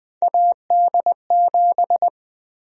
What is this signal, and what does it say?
Morse code, heard as a keyed tone, about 21 wpm, 700 Hz. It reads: AD7
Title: USA Call Sign Prefixes - 20wpm